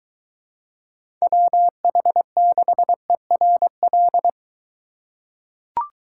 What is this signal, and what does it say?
Morse code, heard as a keyed tone, about 22 wpm, 700 Hz. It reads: WH6ERL